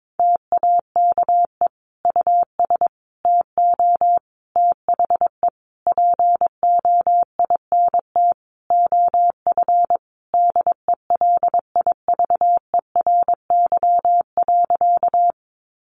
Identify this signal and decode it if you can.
Morse code, heard as a keyed tone, about 22 wpm, 700 Hz. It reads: TAKE US TO THE POINT OF DELIVERY.
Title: Sentences from Top 1500 Words 22wpm